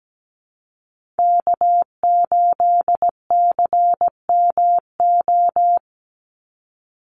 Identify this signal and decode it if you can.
Morse code, heard as a keyed tone, about 17 wpm, 700 Hz. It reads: K8CMO
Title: Call Signs - 17wpm